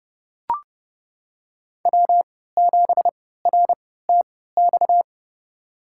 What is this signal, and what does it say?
Morse code, heard as a keyed tone, about 30 wpm, spaced as for 17 wpm, 700 Hz. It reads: W7RTX
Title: USA Call Signs - 3x Character Spacing - 30wpm